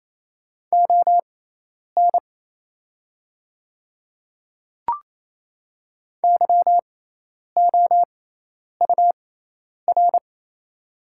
Morse code, heard as a keyed tone, about 27 wpm, spaced as for 10 wpm, 690 Hz. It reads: ON YOUR
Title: 6x Character Spacing - Top 100 Words - 28wpm